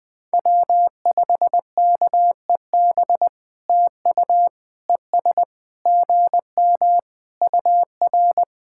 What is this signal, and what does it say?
Morse code, heard as a keyed tone, about 20 wpm, 700 Hz. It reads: W5KEB TU ES GM UR
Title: POTA QSO Practice - 20wpm